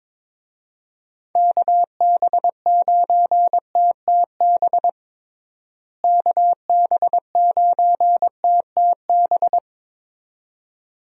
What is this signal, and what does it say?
Morse code, heard as a keyed tone, about 22 wpm, 700 Hz. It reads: KB9TTB KB9TTB
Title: USA Call Signs - 6 Characters - Repeated 2x - 22wpm